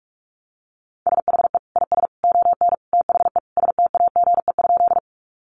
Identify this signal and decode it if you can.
Morse code, heard as a keyed tone, about 44 wpm, 700 Hz. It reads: SHE IS ON THE STAGE?
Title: Sentences from Top 1200 Words 45wpm